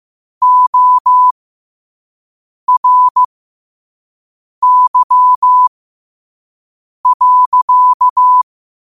Morse code, heard as a keyed tone, about 15 wpm, spaced as for 5 wpm, 1000 Hz. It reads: ORY.
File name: qotdfiles / 5wpm